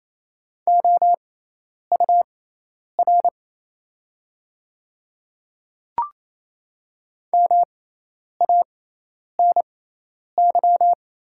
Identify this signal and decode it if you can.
Morse code, heard as a keyed tone, about 28 wpm, spaced as for 10 wpm, 700 Hz. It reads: OUR MANY
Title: 6x Character Spacing - Top 200 Words - 28wpm